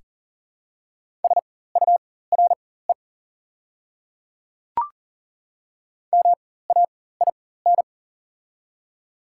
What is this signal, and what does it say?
Morse code, heard as a keyed tone, about 40 wpm, spaced as for 19 wpm, 690 Hz. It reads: SURE MAIN
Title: 4x Character Spacing - Top 500 Words - 40wpm